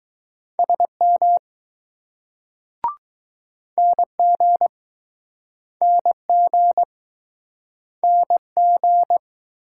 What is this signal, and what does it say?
Morse code, heard as a keyed tone, about 22 wpm, 700 Hz. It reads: SM NG NG NG